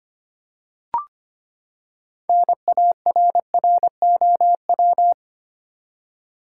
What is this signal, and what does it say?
Morse code, heard as a keyed tone, about 25 wpm, 700 Hz. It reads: NARROW
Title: Six Letter Words - 25wpm